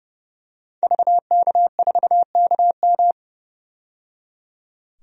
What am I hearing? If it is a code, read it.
Morse code, heard as a keyed tone, about 30 wpm, 700 Hz. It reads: VK4KM
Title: DXCC-100 Call Signs - 5 Characters - 30wpm